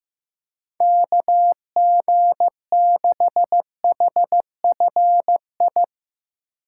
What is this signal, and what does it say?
Morse code, heard as a keyed tone, about 15 wpm, 700 Hz. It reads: KG6HFI
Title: USA Call Signs - 6 Characters - 6x Character Spacing - 15wpm